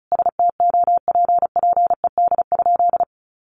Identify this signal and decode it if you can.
Morse code, heard as a keyed tone, about 34 wpm, 700 Hz. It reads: STOPPED?
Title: A Sound Of Thunder 35wpm